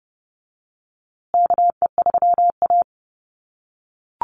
Morse code, heard as a keyed tone, about 30 wpm, 700 Hz. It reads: KE3A
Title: USA Call Signs Repeated 2x 30wpm